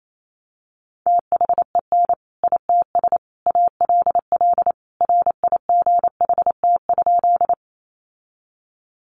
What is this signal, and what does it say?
Morse code, heard as a keyed tone, about 28 wpm, 700 Hz. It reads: THEN ITS ALL RIGHT?
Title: Sentences from Top 600 Words 28wpm